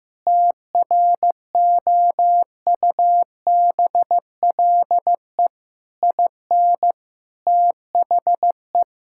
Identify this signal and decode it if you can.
Morse code, heard as a keyed tone, about 15 wpm, 700 Hz. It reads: TROUBLE IN THE
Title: Sentences from Top 1600 Words 15wpm